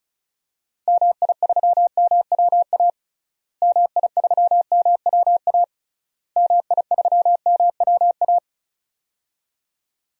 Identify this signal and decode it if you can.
Morse code, heard as a keyed tone, about 34 wpm, 700 Hz. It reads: MI3MWA MI3MWA MI3MWA
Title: DXCC-100 Call Signs - 6 Characters - Repeated 3x - 35wpm